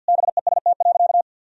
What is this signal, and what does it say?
Morse code, heard as a keyed tone, about 50 wpm, 700 Hz. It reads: BEST.
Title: Sentences from Top 1000 Words - Review Part 3 - 50wpm